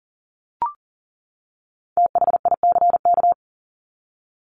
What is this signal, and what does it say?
Morse code, heard as a keyed tone, about 40 wpm, 700 Hz. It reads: THICK
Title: Top 1501-1750 Words 40wpm